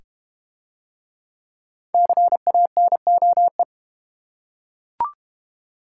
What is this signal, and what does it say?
Morse code, heard as a keyed tone, about 28 wpm, 700 Hz. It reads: CANOE
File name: Easy Words - Speed-Racing with Hints - 25wpm